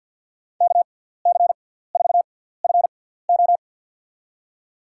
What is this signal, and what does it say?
Morse code, heard as a keyed tone, about 47 wpm, spaced as for 17 wpm, 700 Hz. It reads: KC4FO
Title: USA Call Signs - 6x Character Spacing - 50wpm